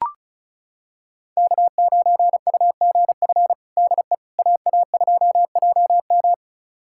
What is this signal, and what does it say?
Morse code, heard as a keyed tone, about 36 wpm, 700 Hz. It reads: K9UGF DE AA2JM
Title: USA Callsign DE Callsign - 5 Characters - 35wpm